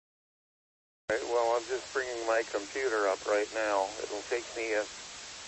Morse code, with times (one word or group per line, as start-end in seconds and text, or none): none